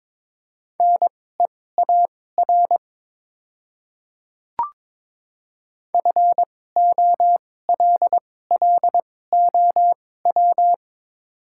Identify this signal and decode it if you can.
Morse code, heard as a keyed tone, about 22 wpm, spaced as for 16 wpm, 700 Hz. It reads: NEAR FOLLOW
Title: 2x Character Spacing - Top 300 Words - 22wpm